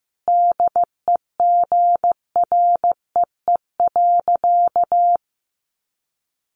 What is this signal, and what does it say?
Morse code, heard as a keyed tone, about 14 wpm, 700 Hz. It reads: DEGREE.
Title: Sentences from Top 1500 Words 15wpm